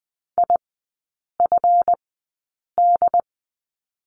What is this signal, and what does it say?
Morse code, heard as a keyed tone, about 20 wpm, spaced as for 8 wpm, 700 Hz. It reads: IFD
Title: 152 Lesson 25e - Letter p - Three Characters 20wpm